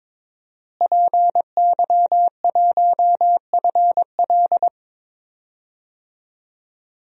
Morse code, heard as a keyed tone, about 22 wpm, 700 Hz. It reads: PY1FL
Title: DXCC-100 Call Signs - 5 Characters - Repeated 2x - 22wpm